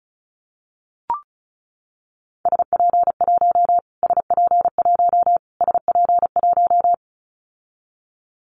Morse code, heard as a keyed tone, about 34 wpm, 700 Hz. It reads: SP1 SP1 SP1
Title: Callsign Prefixes for Top 25 DXCC - Repeated 3x - 35wpm